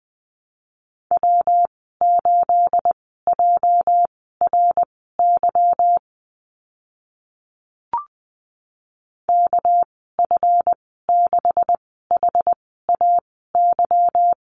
Morse code, heard as a keyed tone, about 20 wpm, spaced as for 15 wpm, 700 Hz. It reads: W8JRY KF6HAY
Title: 213 Lesson 34g - Number 8 - Call Signs 20wpm